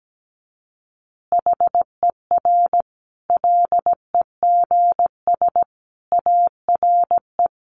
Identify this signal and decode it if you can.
Morse code, heard as a keyed tone, about 17 wpm, 700 Hz. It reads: HER LEGS ARE
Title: Easy Phrases 17wpm